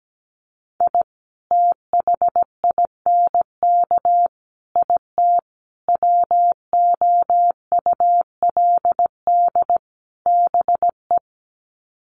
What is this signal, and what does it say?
Morse code, heard as a keyed tone, about 17 wpm, 700 Hz. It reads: I THINK IT WOULD BE
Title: Sets of 5 Words 17wpm